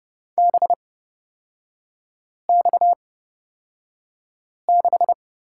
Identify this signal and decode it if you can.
Morse code, heard as a keyed tone, about 30 wpm, spaced as for 5 wpm, 700 Hz. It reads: BX6
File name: Lesson 37e - Letter x - Three Characters 30wpm